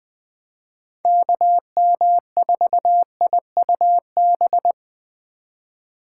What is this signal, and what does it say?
Morse code, heard as a keyed tone, about 20 wpm, 700 Hz. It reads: KM4IUB
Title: USA Call Signs - 6 Characters - 2x Character Spacing - 20wpm